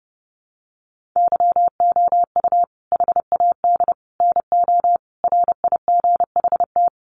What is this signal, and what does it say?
Morse code, heard as a keyed tone, about 30 wpm, 700 Hz. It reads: YOU HAD NO RIGHT
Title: Sentences from Top 1300 Words 30wpm